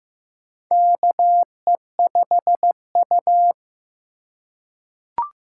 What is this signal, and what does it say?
Morse code, heard as a keyed tone, about 15 wpm, 700 Hz. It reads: KE5U
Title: USA Call Signs - 4 Characters - 4x Character Spacing - 15wpm